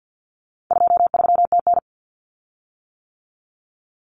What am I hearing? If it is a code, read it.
Morse code, heard as a keyed tone, about 50 wpm, 700 Hz. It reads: 23TN
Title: Common Cut Numbers - 4-Digit 50wpm